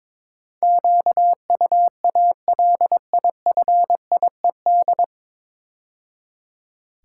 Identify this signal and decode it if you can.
Morse code, heard as a keyed tone, about 22 wpm, 700 Hz. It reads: QUALIFIED
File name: 9 Letter Words - First 3 Letters Given - 22wpm